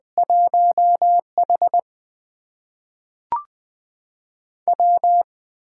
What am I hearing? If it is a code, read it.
Morse code, heard as a keyed tone, about 20 wpm, 700 Hz. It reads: 1H W
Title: USA Call Signs - 4 Characters - 8x Character Spacing - 20wpm